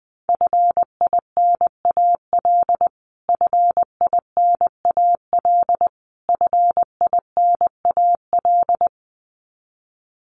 Morse code, heard as a keyed tone, about 19 wpm, 700 Hz. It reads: FINAL FINAL FINAL
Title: Top 500 Words Repeated 20wpm